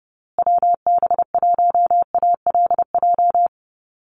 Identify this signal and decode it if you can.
Morse code, heard as a keyed tone, about 30 wpm, 700 Hz. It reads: WB1ALJ
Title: USA Call Signs Repeated 2x 30wpm